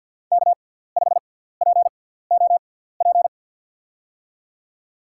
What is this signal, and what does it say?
Morse code, heard as a keyed tone, about 47 wpm, spaced as for 17 wpm, 700 Hz. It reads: K5POP
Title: USA Call Signs - 5 Characters - 6x Character Spacing - 50wpm